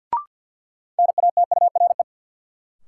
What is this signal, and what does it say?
Morse code, heard as a keyed tone, about 51 wpm, 700 Hz. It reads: NATURE
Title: Sentences from Top 600 Words - New Word Drill - 50wpm